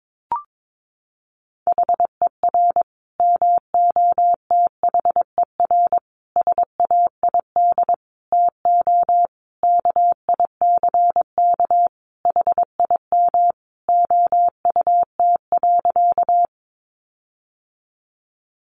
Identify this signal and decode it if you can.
Morse code, heard as a keyed tone, about 22 wpm, 700 Hz. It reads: HER MOTHER SAID TO KICK HIM OUT.
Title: Sentences from Top 1300 Words 22wpm